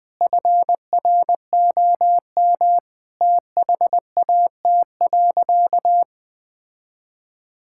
Morse code, heard as a keyed tone, about 20 wpm, 700 Hz. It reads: FROM THAT.
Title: Sentences from Top 1000 Words - Review Part 5 - 20wpm